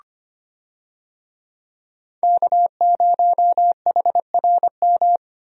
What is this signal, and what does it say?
Morse code, heard as a keyed tone, about 25 wpm, 700 Hz. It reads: K0HRM
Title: USA Call Signs - 5 Characters - Repeated 2x - 25wpm